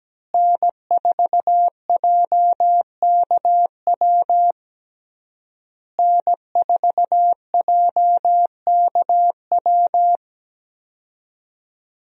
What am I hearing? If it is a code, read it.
Morse code, heard as a keyed tone, about 17 wpm, 700 Hz. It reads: N4JKW N4JKW
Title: USA Call Signs - 5 Characters - Repeated 3x - 17wpm